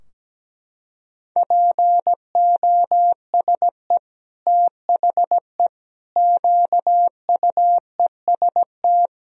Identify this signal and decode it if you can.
Morse code, heard as a keyed tone, about 17 wpm, 700 Hz. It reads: POSE THE QUEST
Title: Sentences from Top 1000 Words - Review Part 2 - 17wpm